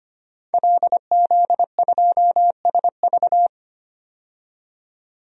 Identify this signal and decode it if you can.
Morse code, heard as a keyed tone, about 25 wpm, 700 Hz. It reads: LZ2SV
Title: DXCC-100 Call Signs - 5 Characters - 25wpm